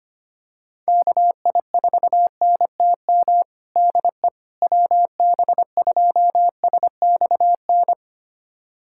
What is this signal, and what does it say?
Morse code, heard as a keyed tone, about 25 wpm, 700 Hz. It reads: KI4NTM DE WB2SXN
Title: USA Callsign DE Callsign 25wpm